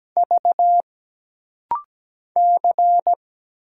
Morse code, heard as a keyed tone, about 17 wpm, 700 Hz. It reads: V C
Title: Single Letters - Rapid-Fire 17wpm